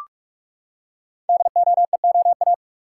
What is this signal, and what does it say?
Morse code, heard as a keyed tone, about 44 wpm, 700 Hz. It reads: DOEOA
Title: Code Groups - 5 Letters - 45wpm